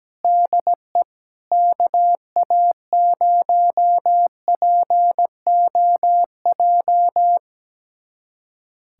Morse code, heard as a keyed tone, about 17 wpm, 700 Hz. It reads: DE KA0POJ
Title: USA Callsign DE Callsign - 6 Characters - 17wpm